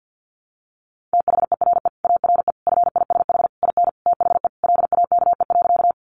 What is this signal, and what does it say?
Morse code, heard as a keyed tone, about 49 wpm, 700 Hz. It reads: THERE ARE FISH IN THE LAKE.
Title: Sentences from Top 1100 Words 50wpm